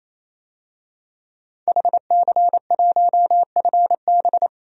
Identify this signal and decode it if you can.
Morse code, heard as a keyed tone, about 27 wpm, 700 Hz. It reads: HC1FB
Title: DXCC-100 Call Signs - 5 Characters - 28wpm